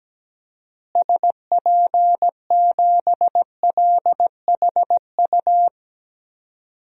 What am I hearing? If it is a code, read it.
Morse code, heard as a keyed tone, about 17 wpm, 700 Hz. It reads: SP7LHU
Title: DXCC-100 Call Signs - 6 Characters - Repeated 3x - 17wpm